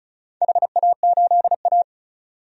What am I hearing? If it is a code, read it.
Morse code, heard as a keyed tone, about 36 wpm, 700 Hz. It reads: HA8A
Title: DXCC-100 Call Signs - 4 Characters - 35wpm